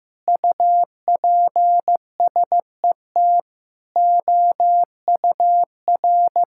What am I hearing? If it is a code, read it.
Morse code, heard as a keyed tone, about 15 wpm, 700 Hz. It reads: UPSET OUR
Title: Sentences from Top 1000 Words - Review Part 6 - 15wpm